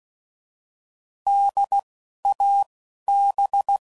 Morse code, heard as a keyed tone, about 16 wpm, 800 Hz. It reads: D A B
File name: Prefixes / By First Letter / Ds